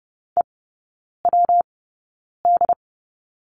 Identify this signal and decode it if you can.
Morse code, heard as a keyed tone, about 30 wpm, spaced as for 9 wpm, 700 Hz. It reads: EWD
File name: Lesson 19e - Letter w - Three Characters 30wpm